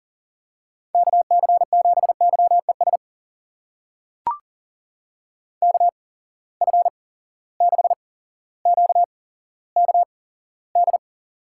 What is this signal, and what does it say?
Morse code, heard as a keyed tone, about 39 wpm, spaced as for 17 wpm, 700 Hz. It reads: KC7YES KF6QKD
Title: USA Call Signs - 6 Characters - 8x Character Spacing - 40wpm